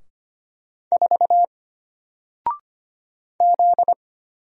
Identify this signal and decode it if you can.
Morse code, heard as a keyed tone, about 25 wpm, 700 Hz. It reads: 4 Z
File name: Single-Letter-Number-25wpm